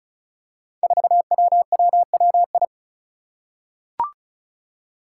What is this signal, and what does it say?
Morse code, heard as a keyed tone, about 36 wpm, 700 Hz. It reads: 4WWWI
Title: Five Challenging Letter-Number Combinations 35wpm